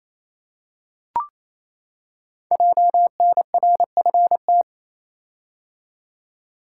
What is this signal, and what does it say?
Morse code, heard as a keyed tone, about 29 wpm, 700 Hz. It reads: JNRFT
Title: Code Groups - 5 Letter:Numbers - 28wpm